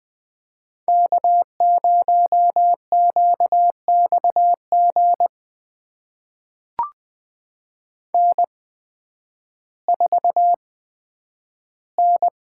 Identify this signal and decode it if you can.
Morse code, heard as a keyed tone, about 20 wpm, spaced as for 11 wpm, 700 Hz. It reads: K0QXG N4N
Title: USA Call Signs - 8x Character Spacing - 20wpm